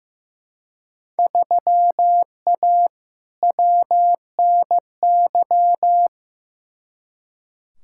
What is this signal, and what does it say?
Morse code, heard as a keyed tone, about 15 wpm, 700 Hz. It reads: 3A WNY
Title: Field Day Exchanges - 15wpm